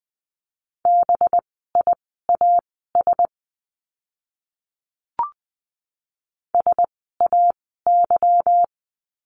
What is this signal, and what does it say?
Morse code, heard as a keyed tone, about 20 wpm, spaced as for 15 wpm, 700 Hz. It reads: BIAS SAY
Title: 185 Lesson 30f - Letter b - Words 20wpm